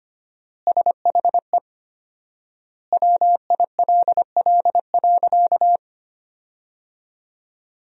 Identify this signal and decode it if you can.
Morse code, heard as a keyed tone, about 25 wpm, 700 Hz. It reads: SHE WILL.
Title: 4x Word-Spacing - Sentences from Top 500 Words - 25wpm